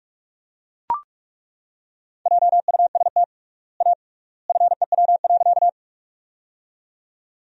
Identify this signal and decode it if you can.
Morse code, heard as a keyed tone, about 42 wpm, 700 Hz. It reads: JUST A FEW.
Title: 3x Word-Spacing - Sentences from Top 300 Words - 45wpm